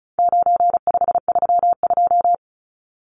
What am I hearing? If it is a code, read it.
Morse code, heard as a keyed tone, about 34 wpm, 700 Hz. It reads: 9532
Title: Numbers - 4-Digit 35wpm